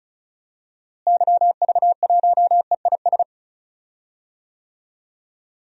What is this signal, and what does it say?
Morse code, heard as a keyed tone, about 34 wpm, 700 Hz. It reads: YV1EIS
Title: DXCC-100 Call Signs - 6 Characters - Repeated 3x - 35wpm